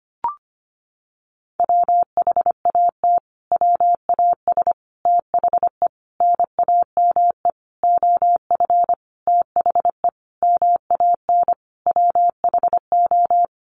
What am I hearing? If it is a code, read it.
Morse code, heard as a keyed tone, about 25 wpm, 700 Hz. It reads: WHAT WAS THE NAME OF THE MAN WHO
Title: Sentences from Top 300 Words 25wpm